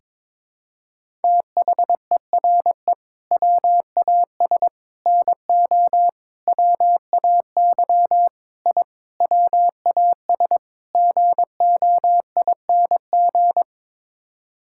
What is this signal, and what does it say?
Morse code, heard as a keyed tone, about 22 wpm, 700 Hz. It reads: THERE WAS NO WAY I WAS GOING
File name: practice / 22wpm / Sets of 7 Words - 22wpm